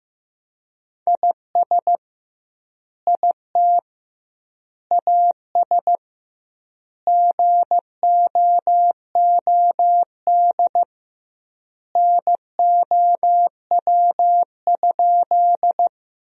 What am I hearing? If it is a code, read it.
Morse code, heard as a keyed tone, about 16 wpm, 700 Hz. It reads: IS IT AS GOOD NOW?
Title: 2x Word-Spacing - Sentences from Top 200 Words - 15wpm